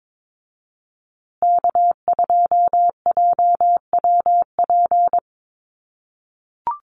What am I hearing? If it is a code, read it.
Morse code, heard as a keyed tone, about 22 wpm, 690 Hz. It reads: K2JWP
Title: USA Call Signs Repeated 3x 22wpm